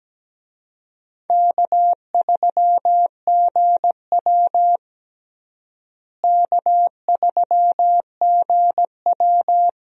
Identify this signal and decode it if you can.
Morse code, heard as a keyed tone, about 17 wpm, 700 Hz. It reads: K3GW K3GW
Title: USA Call Signs - 4 Characters - Repeated 3x - 17wpm